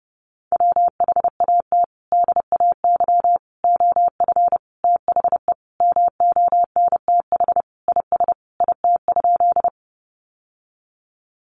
Morse code, heard as a keyed tone, about 30 wpm, 700 Hz. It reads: WHAT DAY OF THE MONTH IS IT?
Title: Sentences from Top 800 Words 30wpm